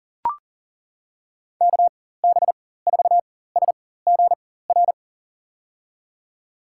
Morse code, heard as a keyed tone, about 37 wpm, spaced as for 18 wpm, 700 Hz. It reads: KB4SGR
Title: USA Call Signs - 6 Characters - 4x Character Spacing - 40wpm